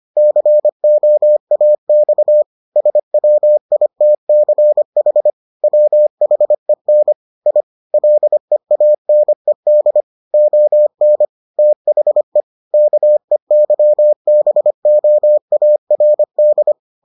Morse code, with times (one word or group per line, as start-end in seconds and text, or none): 0.2-2.4 COAX
2.8-5.3 SWITCH
5.6-7.1 WHEN
7.5-7.6 I
7.9-10.0 LEANED
10.3-11.3 ON
11.6-12.4 THE
12.7-16.7 KEYBOARD